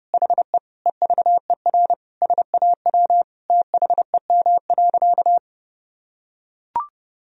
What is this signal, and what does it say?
Morse code, heard as a keyed tone, about 32 wpm, 700 Hz. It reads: HE EVER SAW THEM.
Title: Sentences from Top 1000 Words - Review Part 2 - 30wpm